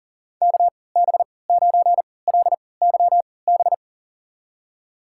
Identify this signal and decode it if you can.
Morse code, heard as a keyed tone, about 37 wpm, spaced as for 22 wpm, 700 Hz. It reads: KB9LYB
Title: USA Call Signs - 6 Characters - 3x Character Spacing - 40wpm